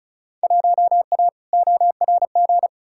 Morse code, heard as a keyed tone, about 34 wpm, 700 Hz. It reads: 1A ORG
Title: Field Day Exchanges - 35wpm